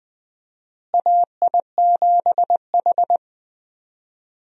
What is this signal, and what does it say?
Morse code, heard as a keyed tone, about 20 wpm, 700 Hz. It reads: AI7H
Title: USA Call Signs - 4 Characters - Repeated 2x - 20wpm